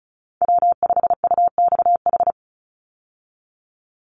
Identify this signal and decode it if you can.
Morse code, heard as a keyed tone, about 34 wpm, 700 Hz. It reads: W5UXH
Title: Call Signs Repeated 2x - 35wpm